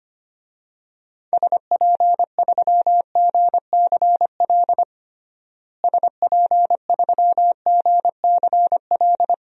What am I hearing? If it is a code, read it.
Morse code, heard as a keyed tone, about 25 wpm, 700 Hz. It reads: SP3GCL SP3GCL
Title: DXCC-100 Call Signs - 6 Characters - Repeated 2x - 25wpm